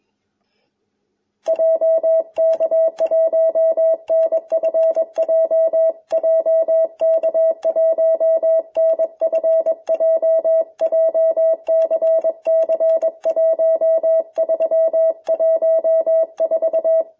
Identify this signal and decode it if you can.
Morse code, heard as a keyed tone, about 22 wpm, 660 Hz. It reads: JK1NFJ JK1NFJ JCC1314